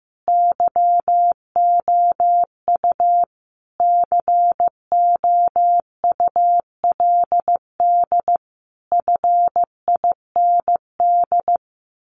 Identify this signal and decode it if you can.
Morse code, heard as a keyed tone, about 15 wpm, 700 Hz. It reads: YOU COULD FIND
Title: Sentences from Top 1400 Words 15wpm